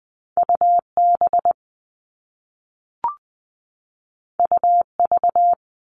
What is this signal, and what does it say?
Morse code, heard as a keyed tone, about 20 wpm, 700 Hz. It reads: UB UV
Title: All Cut Numbers - 2-Digit 20wpm